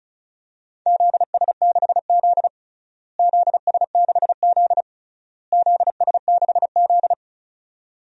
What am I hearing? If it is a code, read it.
Morse code, heard as a keyed tone, about 36 wpm, 690 Hz. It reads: ZS6Z ZS6Z ZS6Z
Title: DXCC-100 Call Signs - 4 Characters - Repeated 3x - 35wpm